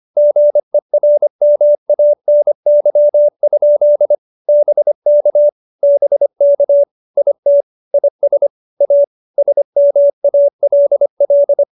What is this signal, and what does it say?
Morse code, heard as a keyed tone, about 25 wpm, 590 Hz. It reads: GERMANY? BK BK IT IS A SMALL
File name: QSO318 25